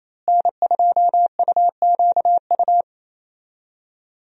Morse code, heard as a keyed tone, about 28 wpm, 700 Hz. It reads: N2UQU
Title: USA Call Signs - 5 Characters - Repeated 2x - 28wpm